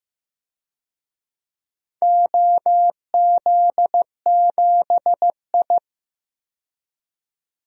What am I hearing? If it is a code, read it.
Morse code, heard as a keyed tone, about 15 wpm, 700 Hz. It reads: OZ7I
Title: DXCC-100 Call Signs - 4 Characters - 15wpm